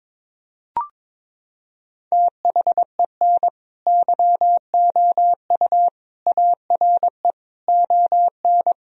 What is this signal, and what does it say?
Morse code, heard as a keyed tone, about 22 wpm, 700 Hz. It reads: THEN YOU ARE ON
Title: Sentences from Top 1000 Words - Review Part 4 - 22wpm